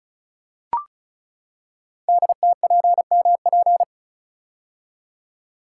Morse code, heard as a keyed tone, about 36 wpm, 700 Hz. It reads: DTPMP
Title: Code Groups - 5 Letters - 35wpm